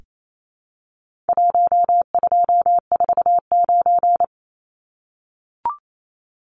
Morse code, heard as a keyed tone, about 28 wpm, 700 Hz. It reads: 1249
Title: Numbers - 4-Digit 28wpm